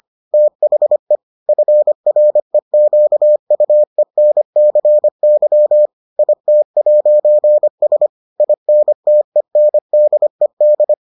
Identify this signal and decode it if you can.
Morse code, heard as a keyed tone, about 25 wpm, 600 Hz. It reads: THE FREQUENCY IT'S INTENDED